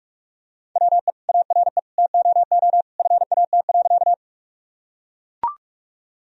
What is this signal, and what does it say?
Morse code, heard as a keyed tone, about 44 wpm, 700 Hz. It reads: WE ARE TOO FAT.
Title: Sentences from Top 1000 Words - Review Part 4 - 45wpm